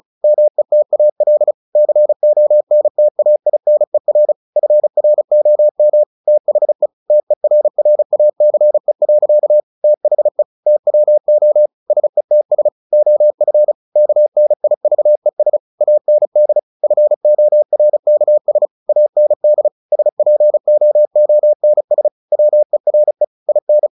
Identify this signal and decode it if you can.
Morse code, heard as a keyed tone, about 34 wpm, 600 Hz. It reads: METAL CONTAINER FROM THE TERRACE. THE TWO SETS OF KNIVES AND FORKS AND SPOONS WERE IN
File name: the old man and the sea / the old man and the sea 1 35wpm